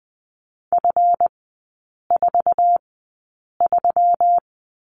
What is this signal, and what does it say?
Morse code, heard as a keyed tone, about 20 wpm, spaced as for 8 wpm, 700 Hz. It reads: F43
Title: 159 Lesson 26e - Letter g - Three Characters 20wpm